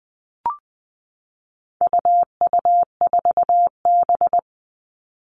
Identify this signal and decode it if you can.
Morse code, heard as a keyed tone, about 20 wpm, 700 Hz. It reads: UU4B
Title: All Cut Numbers - 4-Digit 20wpm